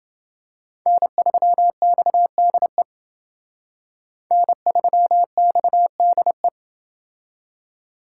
Morse code, heard as a keyed tone, about 28 wpm, 700 Hz. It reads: N3XDE N3XDE